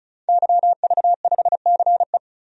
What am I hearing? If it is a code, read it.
Morse code, heard as a keyed tone, about 35 wpm, 700 Hz. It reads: YV5CE
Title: DXCC-100 Call Signs - 5 Characters - 35wpm